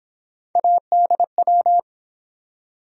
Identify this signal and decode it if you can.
Morse code, heard as a keyed tone, about 29 wpm, 700 Hz. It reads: ADW